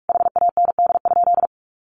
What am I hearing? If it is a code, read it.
Morse code, heard as a keyed tone, about 44 wpm, 700 Hz. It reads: HAND?
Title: Sentences from Top 200 Words 45wpm